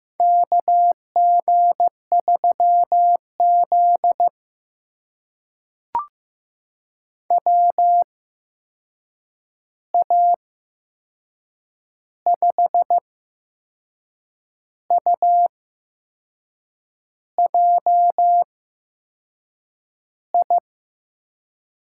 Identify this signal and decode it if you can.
Morse code, heard as a keyed tone, about 15 wpm, spaced as for 6 wpm, 700 Hz. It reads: KG3Z WA5UJI
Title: USA Call Signs - 8x Character Spacing - 15wpm